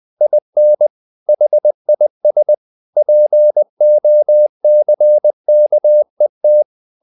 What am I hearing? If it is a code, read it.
Morse code, heard as a keyed tone, about 20 wpm, 600 Hz. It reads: IN HIS POCKET
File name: the old man and the sea / the old man and the sea part 1 20wpm